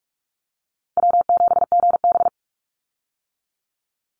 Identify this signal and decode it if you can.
Morse code, heard as a keyed tone, about 44 wpm, 700 Hz. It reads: W7GB
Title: Easy Call Signs - 45wpm